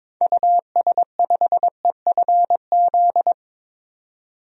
Morse code, heard as a keyed tone, about 22 wpm, 700 Hz. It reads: US5EFZ
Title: DXCC-100 Call Signs - 6 Characters - 22wpm